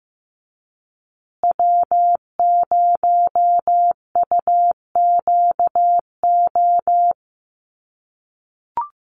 Morse code, heard as a keyed tone, about 15 wpm, 700 Hz. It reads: W0UQO
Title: USA Call Signs Repeated 2x 15wpm